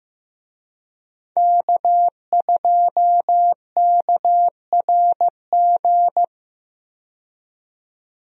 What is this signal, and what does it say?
Morse code, heard as a keyed tone, about 15 wpm, 700 Hz. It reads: K2KRG
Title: USA Call Signs - 5 Characters - 15wpm